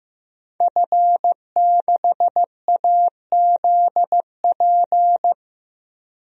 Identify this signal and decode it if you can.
Morse code, heard as a keyed tone, about 15 wpm, 700 Hz. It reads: F6AZP
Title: DXCC-100 Call Signs - 5 Characters - Repeated 2x - 15wpm